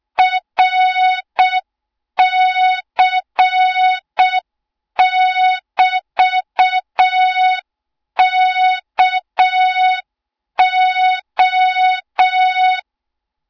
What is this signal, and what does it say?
Morse code, heard as a keyed tone, about 6 wpm, 3700 Hz. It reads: RC<BT>KO
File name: Morze-HA2MN / hang / 20 GYAKORLAT